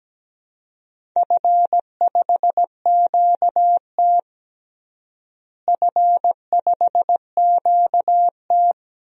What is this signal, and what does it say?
Morse code, heard as a keyed tone, about 17 wpm, 700 Hz. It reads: F5QT F5QT
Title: DXCC-100 Call Signs - 4 Characters - Repeated 3x - 17wpm